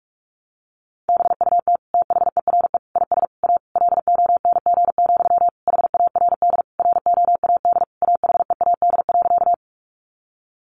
Morse code, heard as a keyed tone, about 44 wpm, 700 Hz. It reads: BUT THERE IS A LONG, HARD ROAD AHEAD.
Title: Sentences from Top 1100 Words 45wpm